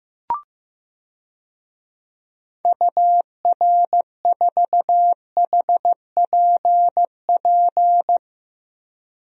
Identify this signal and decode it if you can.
Morse code, heard as a keyed tone, about 15 wpm, 700 Hz. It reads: UR4HPP
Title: DXCC-100 Call Signs - 6 Characters - Repeated 3x - 15wpm